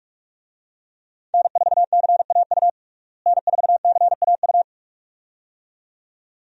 Morse code, heard as a keyed tone, about 48 wpm, 700 Hz. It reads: N4CAU N4CAU
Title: USA Call Signs - 5 Characters - Repeated 2x - 45wpm